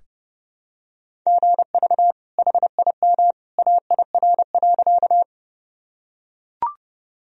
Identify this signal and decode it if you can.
Morse code, heard as a keyed tone, about 30 wpm, 700 Hz. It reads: GV HIM AIR.
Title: Sentences from Top 200 Words and Abbreviations 30wpm